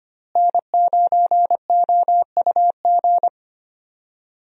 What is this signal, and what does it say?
Morse code, heard as a keyed tone, about 25 wpm, 700 Hz. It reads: N9OUG
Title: USA Call Signs - 5 Characters - Repeated 2x - 25wpm